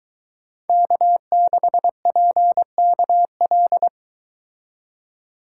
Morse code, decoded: K6PKL

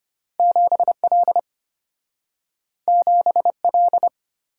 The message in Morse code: 7L 7L

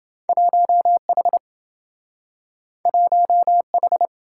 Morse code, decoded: 1H 1H

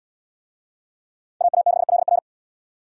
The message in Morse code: AA6LR

